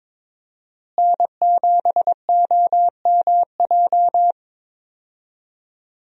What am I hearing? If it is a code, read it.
Morse code, heard as a keyed tone, about 22 wpm, 700 Hz. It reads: N7OMJ
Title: USA Call Signs - 5 Characters - Repeated 2x - 22wpm